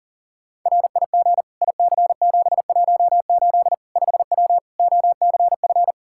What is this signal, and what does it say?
Morse code, heard as a keyed tone, about 40 wpm, 700 Hz. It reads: RIG IC718 5W OCF
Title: K1USN SST QSO Practice 40wpm